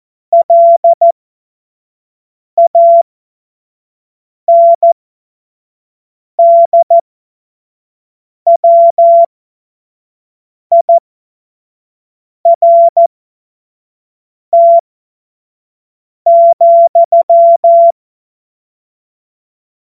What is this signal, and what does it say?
Morse code, heard as a keyed tone, about 14 wpm, spaced as for 5 wpm, 680 Hz. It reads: LANDWIRT,
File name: Witz-12-August-2025-25-F70